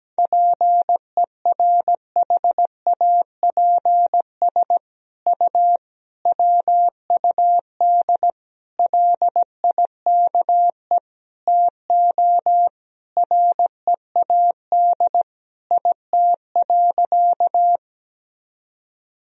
Morse code, heard as a keyed tone, about 17 wpm, 700 Hz. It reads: PERHAPS U WUD LIKE TO READ IT.
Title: Sentences from Top 500 Words and Abbreviations 17wpm